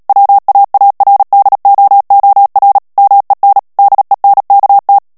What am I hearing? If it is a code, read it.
Morse code, heard as a keyed tone, about 37 wpm, 800 Hz. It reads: WAARDOOR MEN DENKT
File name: morse mp3 / HOLLANDAIS texte clair 2 vitesse 35 mots minute